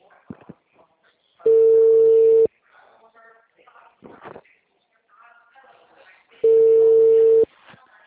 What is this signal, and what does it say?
Ringback tone (China, 450 Hz), with speech.